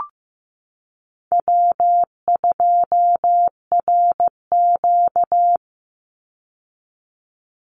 Morse code, heard as a keyed tone, about 15 wpm, 700 Hz. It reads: W2RQ
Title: Call Signs Repeated 2x - 15wpm